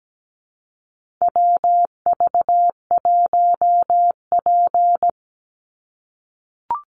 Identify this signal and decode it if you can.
Morse code, heard as a keyed tone, about 17 wpm, 700 Hz. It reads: WV1P
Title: USA Call Signs 17wpm